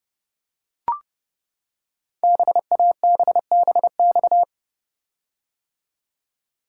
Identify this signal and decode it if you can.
Morse code, heard as a keyed tone, about 30 wpm, 700 Hz. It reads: BABBX
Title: Code Groups - 5 Letters - 30wpm